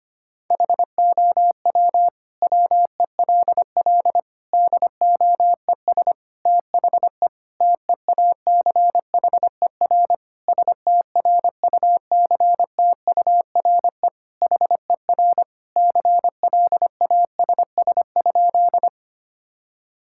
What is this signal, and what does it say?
Morse code, heard as a keyed tone, about 25 wpm, 700 Hz. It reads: HOW WELL DOES THE TEACHER STRUCTURE HER CLASS?
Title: Sentences from Top 1000 Words - Review Part 6 - 25wpm